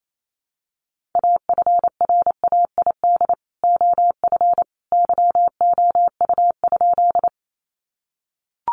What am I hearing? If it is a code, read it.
Morse code, heard as a keyed tone, about 28 wpm, 700 Hz. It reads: AFRAID OF YOU?
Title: Sentences from Top 1500 Words 28wpm